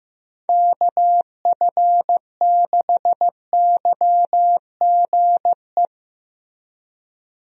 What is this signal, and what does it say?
Morse code, heard as a keyed tone, about 16 wpm, 700 Hz. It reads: KF6YGE
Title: USA Call Signs - 6 Characters - Repeated 2x - 15wpm